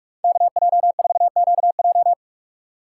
Morse code, heard as a keyed tone, about 46 wpm, 700 Hz. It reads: KJ4QJ
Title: USA Call Signs - 5 Characters - 4x Character Spacing - 45wpm